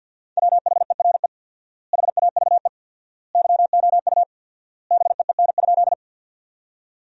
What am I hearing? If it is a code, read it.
Morse code, heard as a keyed tone, about 50 wpm, 700 Hz. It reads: WHERE HAVE YOU BEEN?
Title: 4x Word-Spacing - Sentences from Top 200 Words - 50wpm